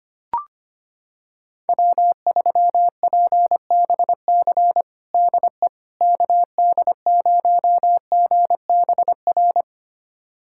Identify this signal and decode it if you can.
Morse code, heard as a keyed tone, about 25 wpm, 700 Hz. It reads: W3PBC DE KD0GBR
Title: USA Callsign DE Callsign 25wpm